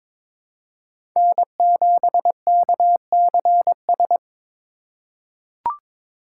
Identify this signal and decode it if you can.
Morse code, heard as a keyed tone, about 22 wpm, 700 Hz. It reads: N7KCS
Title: USA Call Signs - 5 Characters - 3x Character Spacing - 22wpm